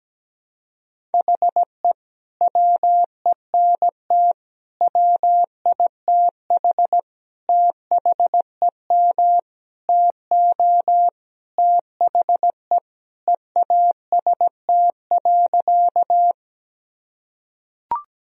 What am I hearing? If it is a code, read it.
Morse code, heard as a keyed tone, about 17 wpm, 700 Hz. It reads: HE WENT WITH THEM TO THE EAST.
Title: Sentences from Top 1000 Words - Review Part 3 - 17wpm